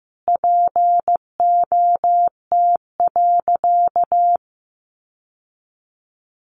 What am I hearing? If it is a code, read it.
Morse code, heard as a keyed tone, about 15 wpm, 700 Hz. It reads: POT.
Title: Sentences from Top 1000 Words 15wpm